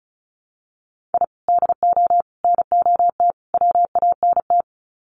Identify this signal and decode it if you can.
Morse code, heard as a keyed tone, about 34 wpm, 700 Hz. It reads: I DO NOT WANT
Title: Sets of 4 Words 35wpm